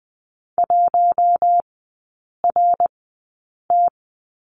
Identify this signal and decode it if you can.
Morse code, heard as a keyed tone, about 20 wpm, spaced as for 8 wpm, 700 Hz. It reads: 1RT
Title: 053 Lesson 10e - Letter r - Three Characters 20wpm